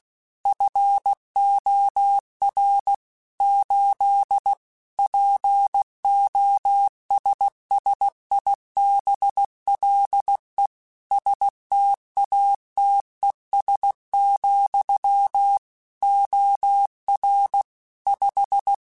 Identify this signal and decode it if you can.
Morse code, heard as a keyed tone, about 16 wpm, 800 Hz. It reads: FOR 8 POSSIBLE STATES, OR 5
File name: Technical Subjects / Technical Subjects-mp3s / AND Gate @15